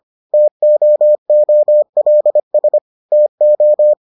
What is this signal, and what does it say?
Morse code, heard as a keyed tone, about 25 wpm, 600 Hz. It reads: TOOLS TO